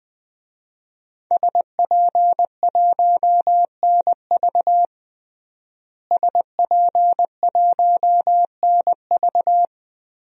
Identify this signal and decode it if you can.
Morse code, heard as a keyed tone, about 20 wpm, 700 Hz. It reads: SP1NV SP1NV
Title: DXCC-100 Call Signs - 5 Characters - Repeated 3x - 20wpm